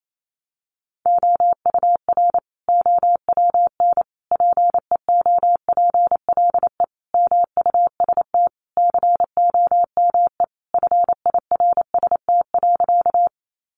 Morse code, heard as a keyed tone, about 28 wpm, 700 Hz. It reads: OUR OWN PEOPLE MUST COME FIRST.
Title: Sentences from Top 300 Words 28wpm